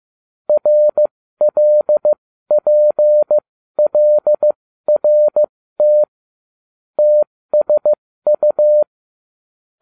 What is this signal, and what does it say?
Morse code, heard as a keyed tone, about 15 wpm, spaced as for 13 wpm, 600 Hz. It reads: RLPLRT TSU